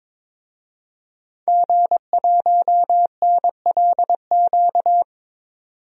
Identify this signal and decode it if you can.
Morse code, heard as a keyed tone, about 22 wpm, 700 Hz. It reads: G1NLQ
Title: DXCC-100 Call Signs - 5 Characters - 22wpm